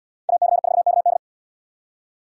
Morse code, heard as a keyed tone, about 69 wpm, 700 Hz. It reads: AC4KD